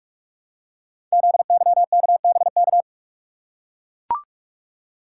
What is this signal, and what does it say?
Morse code, heard as a keyed tone, about 44 wpm, 700 Hz. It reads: ZYKBK
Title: Code Groups - 5 Letters - 45wpm